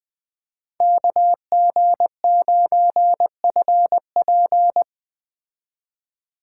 Morse code, heard as a keyed tone, about 20 wpm, 700 Hz. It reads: KG9FP
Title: Speed Racing - USA Call Signs 17wpm